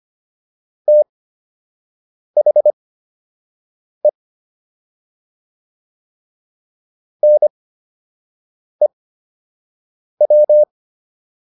Morse code, heard as a keyed tone, about 25 wpm, spaced as for 6 wpm, 600 Hz. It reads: THE NEW